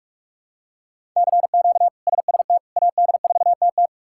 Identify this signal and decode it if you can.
Morse code, heard as a keyed tone, about 46 wpm, 700 Hz. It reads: CQ SST AD4TT
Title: K1USN SST QSO Practice - 45wpm